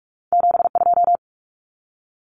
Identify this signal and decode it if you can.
Morse code, heard as a keyed tone, about 46 wpm, 700 Hz. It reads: 72
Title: Common Cut Numbers - 2-Digit 45wpm